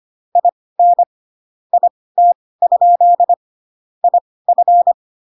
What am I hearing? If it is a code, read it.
Morse code, heard as a keyed tone, about 25 wpm, spaced as for 18 wpm, 700 Hz. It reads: IN IT? IF
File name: Pinocchio / Pinocchio ch1 12 25 ik1hge